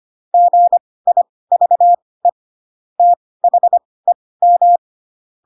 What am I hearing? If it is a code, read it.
Morse code, heard as a keyed tone, about 25 wpm, spaced as for 18 wpm, 700 Hz. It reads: GIVE THEM
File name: Pinocchio / Pinocchio ch7 12 25 ik1hge